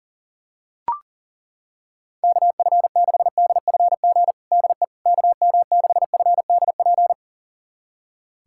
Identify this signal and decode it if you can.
Morse code, heard as a keyed tone, about 40 wpm, 700 Hz. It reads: KF6DFG DE KM6FDP
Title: USA Callsign DE Callsign - 6 Characters - 40wpm